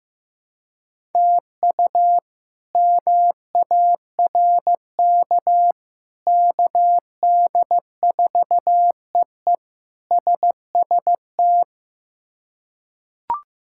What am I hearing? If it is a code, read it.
Morse code, heard as a keyed tone, about 15 wpm, 700 Hz. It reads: TU MARK KD4EE SST
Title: K1USN SST QSO Practice - 15wpm